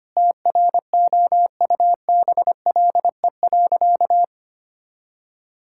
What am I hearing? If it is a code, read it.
Morse code, heard as a keyed tone, about 25 wpm, 700 Hz. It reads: TROUBLE.
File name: practice / 25wpm / Sentences with 9-Word Phrases - 25wpm